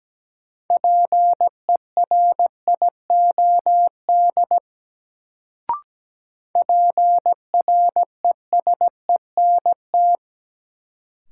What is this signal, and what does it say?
Morse code, heard as a keyed tone, about 17 wpm, 700 Hz. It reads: PERIOD PRESENT
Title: Sentences from Top 800 Words - New Word Drill - 17wpm